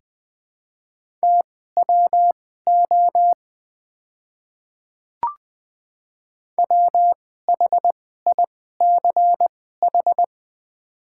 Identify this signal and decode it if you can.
Morse code, heard as a keyed tone, about 21 wpm, spaced as for 15 wpm, 700 Hz. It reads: TWO WHICH
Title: 2x Character Spacing - Top 200 Words - 20wpm